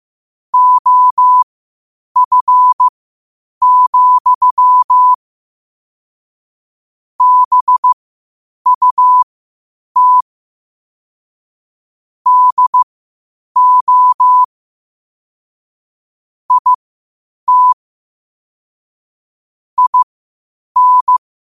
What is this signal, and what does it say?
Morse code, heard as a keyed tone, about 15 wpm, spaced as for 8 wpm, 1000 Hz. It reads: OF, BUT DO IT IN
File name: qotdfiles / 8wpm